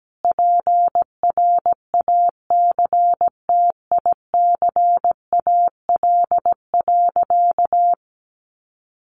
Morse code, heard as a keyed tone, about 17 wpm, 700 Hz. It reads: PRACTICAL.
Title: Sentences from Top 1400 Words 17wpm